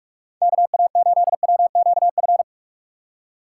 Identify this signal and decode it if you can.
Morse code, heard as a keyed tone, about 44 wpm, 700 Hz. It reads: KA8WQF
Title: USA Call Signs - 6x Character Spacing - 45wpm